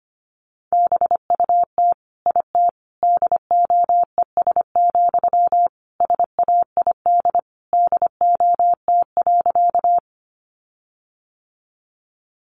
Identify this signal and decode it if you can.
Morse code, heard as a keyed tone, about 25 wpm, 700 Hz. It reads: BUT IT DOES, SAID DOT.
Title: Sentences from Top 1500 Words 25wpm